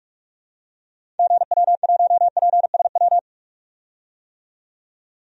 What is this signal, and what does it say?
Morse code, heard as a keyed tone, about 47 wpm, 700 Hz. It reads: GW1PSW
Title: DXCC-100 Call Signs - 6 Characters - 45wpm